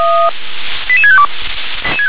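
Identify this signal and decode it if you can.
Five-tone selective call (ZVEI), heard as quick runs of five tones, tones of about 70 ms. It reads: 88652
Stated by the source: ZVEI five-tone selective call 88761 received by SDR.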